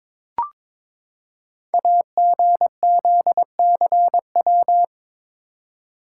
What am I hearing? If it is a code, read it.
Morse code, heard as a keyed tone, about 22 wpm, 700 Hz. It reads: AGZCW
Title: Code Groups - 5 Letters - 22wpm